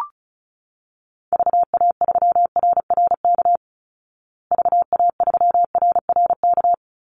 Morse code, heard as a keyed tone, about 34 wpm, 700 Hz. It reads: VA3RRK VA3RRK
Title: Call Signs Repeated 3x - 35wpm